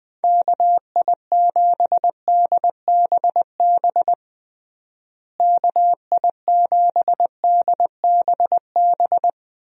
Morse code, heard as a keyed tone, about 20 wpm, 700 Hz. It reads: KI7DBB KI7DBB
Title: USA Call Signs - 6 Characters - Repeated 2x - 20wpm